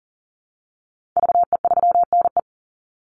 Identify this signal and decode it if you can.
Morse code, heard as a keyed tone, about 40 wpm, 700 Hz. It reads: VE3NE
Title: Call Signs Repeated 2x - 40wpm